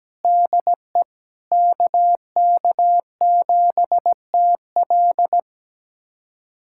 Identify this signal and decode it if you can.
Morse code, heard as a keyed tone, about 17 wpm, 700 Hz. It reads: DE KK7TL
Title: USA Callsign DE Callsign - 5 Characters - 17wpm